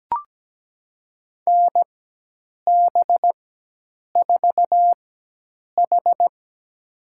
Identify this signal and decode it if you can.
Morse code, heard as a keyed tone, about 17 wpm, spaced as for 8 wpm, 700 Hz. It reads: NB4H
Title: USA Call Signs - 4 Characters - 4x Character Spacing - 17wpm